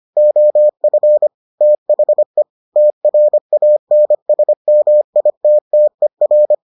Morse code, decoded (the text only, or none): OF THE TRANSMITTER